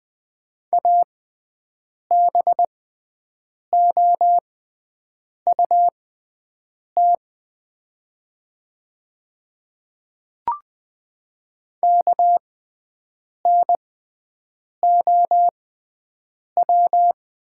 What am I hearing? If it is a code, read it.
Morse code, heard as a keyed tone, about 20 wpm, spaced as for 7 wpm, 700 Hz. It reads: ABOUT KNOW
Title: 6x Character Spacing - Top 100 Words - 20wpm